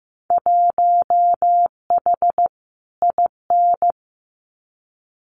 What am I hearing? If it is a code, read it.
Morse code, heard as a keyed tone, about 15 wpm, 700 Hz. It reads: 1H IN
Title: Winter Field Day Exchanges 15wpm